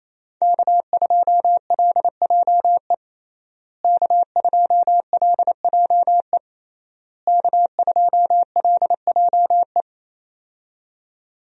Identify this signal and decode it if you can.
Morse code, heard as a keyed tone, about 28 wpm, 700 Hz. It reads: K2LJE K2LJE K2LJE
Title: USA Call Signs - 5 Characters - Repeated 3x - 28wpm